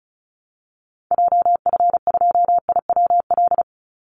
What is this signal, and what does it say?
Morse code, heard as a keyed tone, about 34 wpm, 700 Hz. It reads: JF2IWL
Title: Call Signs Repeated 2x - 35wpm